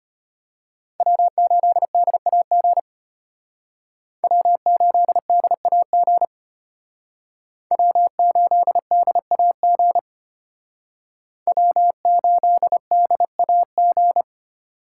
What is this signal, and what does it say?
Morse code, heard as a keyed tone, about 32 wpm, 700 Hz. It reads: W8DAG W8DAG W8DAG W8DAG